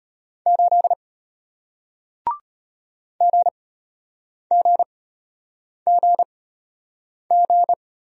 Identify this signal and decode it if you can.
Morse code, heard as a keyed tone, about 34 wpm, spaced as for 8 wpm, 700 Hz. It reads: 8 GGGG